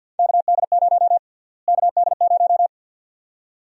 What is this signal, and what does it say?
Morse code, heard as a keyed tone, about 52 wpm, 700 Hz. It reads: KD0 KD0
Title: USA Call Sign Prefixes - Repeated 2x - 50wpm